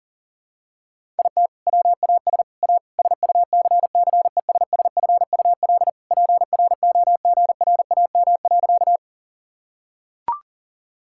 Morse code, heard as a keyed tone, about 40 wpm, 700 Hz. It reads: IT WAS A SUCCESSFUL PROGRAM.
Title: Sentences from Top 1000 Words - Review Part 5 - 40wpm